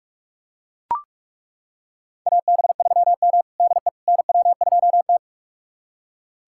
Morse code, heard as a keyed tone, about 41 wpm, 700 Hz. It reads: AB3M DE NW2T
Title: USA Callsign DE Callsign - 4 Characters - 45wpm